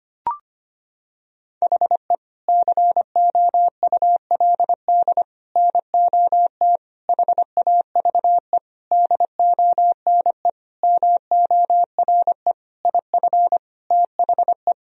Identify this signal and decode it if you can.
Morse code, heard as a keyed tone, about 25 wpm, 700 Hz. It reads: HE COULD NOT HAVE DONE MORE IF THE
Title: Sentences from Top 1000 Words - Review Part 2 - 25wpm